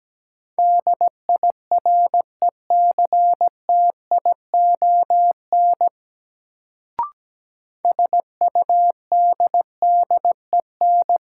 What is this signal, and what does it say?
Morse code, heard as a keyed tone, about 17 wpm, 700 Hz. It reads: DIRECTION SUDDEN
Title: Sets of 4 Words - New Word Drill - 17wpm